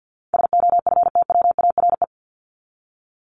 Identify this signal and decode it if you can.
Morse code, heard as a keyed tone, about 50 wpm, 700 Hz. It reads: SOFTWARE
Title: Top 501-750 Words 50wpm